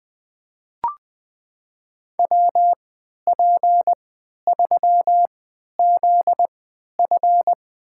Morse code, heard as a keyed tone, about 21 wpm, spaced as for 11 wpm, 700 Hz. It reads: WP3ZF
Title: USA Call Signs - 5 Characters - 3x Character Spacing - 20wpm